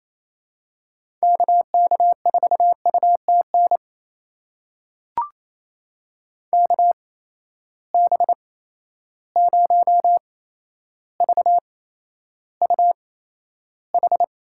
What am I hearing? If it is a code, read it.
Morse code, heard as a keyed tone, about 28 wpm, spaced as for 12 wpm, 700 Hz. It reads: KK4UTN KB0VUH
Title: USA Call Signs - 6 Characters - 8x Character Spacing - 28wpm